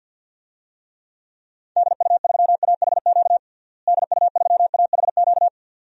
Morse code, heard as a keyed tone, about 52 wpm, 700 Hz. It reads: DU3AHQ DU3AHQ
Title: DXCC-100 Call Signs - 6 Characters - Repeated 2x - 50wpm